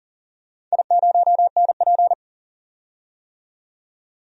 Morse code, heard as a keyed tone, about 40 wpm, 700 Hz. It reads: I0NP
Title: DXCC-100 Call Signs - 4 Characters - 40wpm